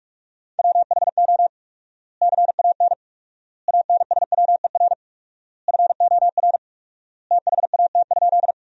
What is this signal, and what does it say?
Morse code, heard as a keyed tone, about 42 wpm, 700 Hz. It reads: WHO CAN ANSWER FOR THAT?
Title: 4x Word-Spacing - Sentences from Top 300 Words - 45wpm